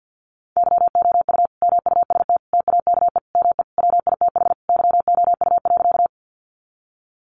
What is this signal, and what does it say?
Morse code, heard as a keyed tone, about 50 wpm, 700 Hz. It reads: YOU MUST TAKE ME WITH YOU.
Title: Sentences from Top 200 Words 50wpm